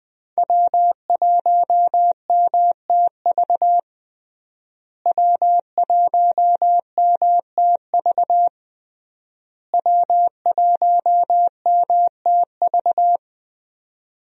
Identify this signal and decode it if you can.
Morse code, heard as a keyed tone, about 20 wpm, 700 Hz. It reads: W1MTV W1MTV W1MTV
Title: USA Call Signs - 5 Characters - Repeated 3x - 20wpm